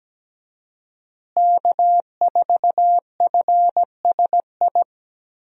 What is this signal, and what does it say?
Morse code, heard as a keyed tone, about 17 wpm, 700 Hz. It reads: K4FSI
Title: USA Call Signs - 5 Characters - Repeated 2x - 17wpm